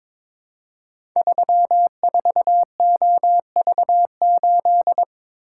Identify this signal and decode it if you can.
Morse code, heard as a keyed tone, about 22 wpm, 700 Hz. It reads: 34OV8
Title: Code Groups - 5 Letter:Numbers - 22wpm